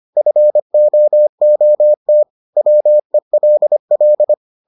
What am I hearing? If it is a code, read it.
Morse code, heard as a keyed tone, about 25 wpm, 600 Hz. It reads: FOOT WELL